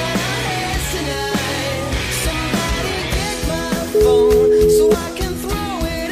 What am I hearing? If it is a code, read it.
Ringback tone (Europe, 425 Hz), with noise.